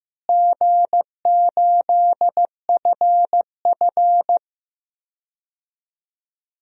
Morse code, heard as a keyed tone, about 16 wpm, 700 Hz. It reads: G8FF
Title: DXCC-100 Call Signs - 4 Characters - Repeated 2x - 15wpm